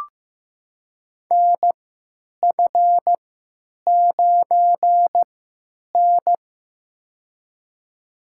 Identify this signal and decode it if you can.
Morse code, heard as a keyed tone, about 16 wpm, spaced as for 9 wpm, 700 Hz. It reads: NF9N
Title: USA Call Signs - 4 Characters - 3x Character Spacing - 15wpm